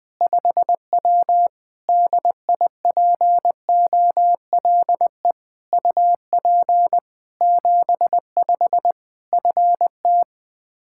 Morse code, heard as a keyed tone, about 20 wpm, 700 Hz. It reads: 5W DIPOLE UP 75 FT
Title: Practice QSO - Exchange 2 - Info - 20wpm